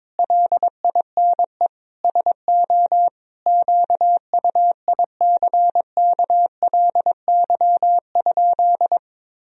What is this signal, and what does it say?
Morse code, heard as a keyed tone, about 22 wpm, 700 Hz. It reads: LINE SO QUICKLY?
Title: Sentences from Top 1000 Words - Review Part 3 - 22wpm